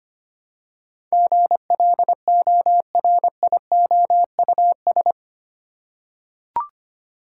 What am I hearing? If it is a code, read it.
Morse code, heard as a keyed tone, about 25 wpm, 700 Hz. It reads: GLORIOUS
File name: Eight Letter Words - 25wpm